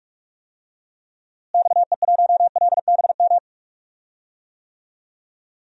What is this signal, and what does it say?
Morse code, heard as a keyed tone, about 45 wpm, 700 Hz. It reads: XE1LBM
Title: DXCC-100 Call Signs - 6 Characters - Repeated 2x - 45wpm